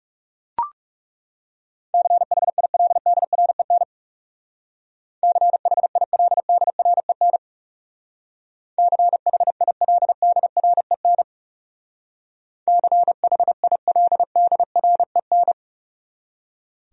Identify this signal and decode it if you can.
Morse code, heard as a keyed tone, about 36 wpm, 700 Hz. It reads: CHILDREN CHILDREN CHILDREN CHILDREN
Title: Speed-Racing Top 300 Words 30wpm